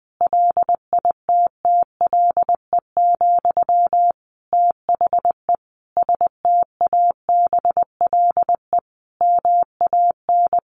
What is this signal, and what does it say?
Morse code, heard as a keyed tone, about 20 wpm, 700 Hz. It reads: LITTLE, THE STABLE MAN
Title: Sentences from Top 1500 Words 20wpm